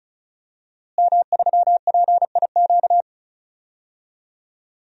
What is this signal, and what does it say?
Morse code, heard as a keyed tone, about 34 wpm, 700 Hz. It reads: M3PIQ
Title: DXCC-100 Call Signs - 5 Characters - Repeated 2x - 35wpm